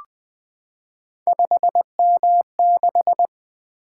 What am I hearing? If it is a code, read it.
Morse code, heard as a keyed tone, about 20 wpm, 700 Hz. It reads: 5M6
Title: Three Letter-Number Combinations - 20wpm